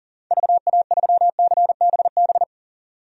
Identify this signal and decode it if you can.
Morse code, heard as a keyed tone, about 40 wpm, 700 Hz. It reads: VA3CBB
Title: DXCC-100 Call Signs - 6 Characters - 40wpm